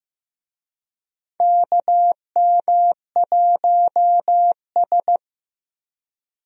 Morse code, heard as a keyed tone, about 16 wpm, 700 Hz. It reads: KM1S
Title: USA Call Signs - 4 Characters - Repeated 3x - 15wpm